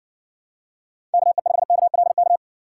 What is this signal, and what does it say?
Morse code, heard as a keyed tone, about 60 wpm, 700 Hz. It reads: KE5KLK